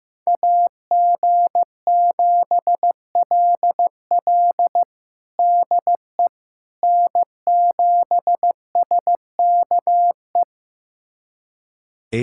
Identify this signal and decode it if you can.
Morse code, heard as a keyed tone, about 15 wpm, 700 Hz. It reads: AG7LL DE N7SKE
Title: USA Callsign DE Callsign - 5 Characters - 15wpm